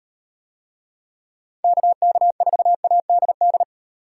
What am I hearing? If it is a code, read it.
Morse code, heard as a keyed tone, about 36 wpm, 700 Hz. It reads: KK4ADD